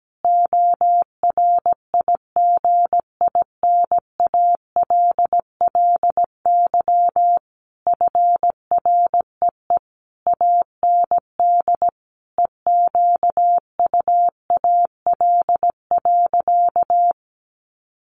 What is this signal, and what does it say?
Morse code, heard as a keyed tone, about 17 wpm, 700 Hz. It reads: ORIGINALLY FREE AND EQUAL.
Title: Sentences from Top 1500 Words 17wpm